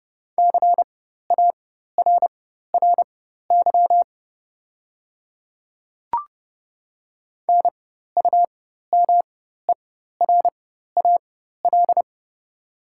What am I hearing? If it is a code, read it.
Morse code, heard as a keyed tone, about 30 wpm, spaced as for 14 wpm, 690 Hz. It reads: CARRY NUMERAL
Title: 4x Character Spacing - Top 500 Words - 30wpm